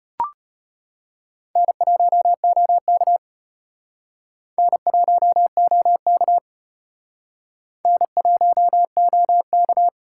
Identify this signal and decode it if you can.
Morse code, heard as a keyed tone, about 33 wpm, 700 Hz. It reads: N1OK N1OK N1OK